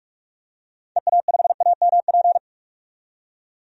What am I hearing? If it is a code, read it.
Morse code, heard as a keyed tone, about 46 wpm, 700 Hz. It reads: EA5AMP
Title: DXCC-100 Call Signs - 6 Characters - 45wpm